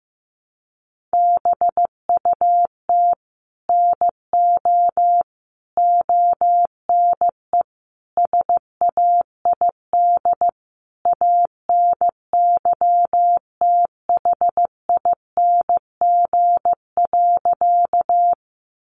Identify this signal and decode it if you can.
Morse code, heard as a keyed tone, about 15 wpm, 700 Hz. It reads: BUT NO ONE SAID ANYTHING.
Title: Sentences from Top 1000 Words 15wpm